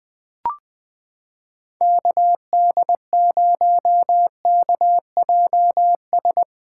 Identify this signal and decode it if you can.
Morse code, heard as a keyed tone, about 19 wpm, 700 Hz. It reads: KD0KJS
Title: USA Callsign DE Callsign - 6 Characters - 20wpm